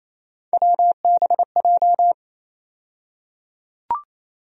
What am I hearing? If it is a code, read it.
Morse code, heard as a keyed tone, about 28 wpm, 700 Hz. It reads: WBJ
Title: USA Call Sign Postfixes - 3 Letters 28wpm